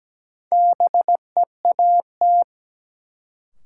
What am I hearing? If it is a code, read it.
Morse code, heard as a keyed tone, about 17 wpm, 700 Hz. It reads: BEAT
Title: Sentences from Top 1300 Words - New Word Drill - 17wpm